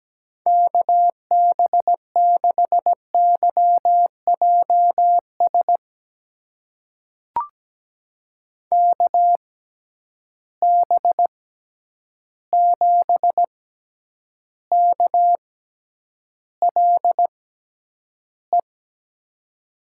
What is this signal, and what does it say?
Morse code, heard as a keyed tone, about 17 wpm, spaced as for 9 wpm, 700 Hz. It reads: KB6YJS KB7KLE
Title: USA Call Signs - 6 Characters - 6x Character Spacing - 17wpm